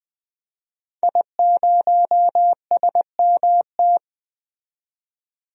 Morse code, heard as a keyed tone, about 20 wpm, 700 Hz. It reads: I0SMT
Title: DXCC-100 Call Signs - 5 Characters - 20wpm